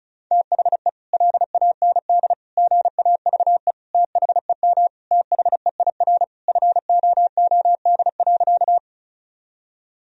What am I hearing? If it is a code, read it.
Morse code, heard as a keyed tone, about 36 wpm, 700 Hz. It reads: THE LAND GAVE THEM THEIR FOOD.
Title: Sentences from Top 1000 Words - Review Part 1 - 35wpm